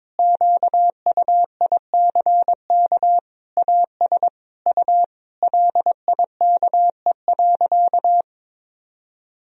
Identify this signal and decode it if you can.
Morse code, heard as a keyed tone, about 22 wpm, 700 Hz. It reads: QUICK AS U LIKE.
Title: Sentences from Top 500 Words and Abbreviations 22wpm